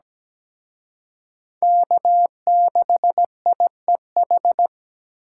Morse code, heard as a keyed tone, about 20 wpm, 700 Hz. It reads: K6IEH
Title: Speed Racing - USA 1x3 Call Signs 17wpm